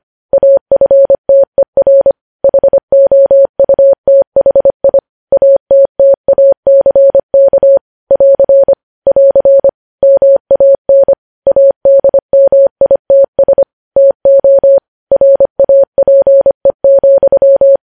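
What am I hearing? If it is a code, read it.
Morse code, heard as a keyed tone, about 25 wpm, 550 Hz. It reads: AFTER HOUTHI ATTACK <AR> <AR> MAN ADMITS TO RAPE,
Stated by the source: Morse code news headlines sent at 25, opening with vvv.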